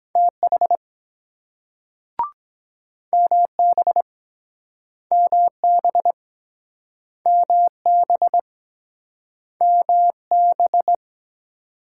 Morse code, heard as a keyed tone, about 22 wpm, 700 Hz. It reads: TH MB MB MB MB